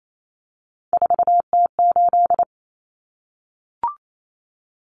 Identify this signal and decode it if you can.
Morse code, heard as a keyed tone, about 28 wpm, 700 Hz. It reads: 4T8
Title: Common Cut Numbers - 3-Digit 28wpm